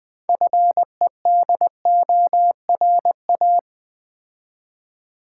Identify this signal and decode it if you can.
Morse code, heard as a keyed tone, about 20 wpm, 700 Hz. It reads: FEDORA
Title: Difficult Words with Hints - 20wpm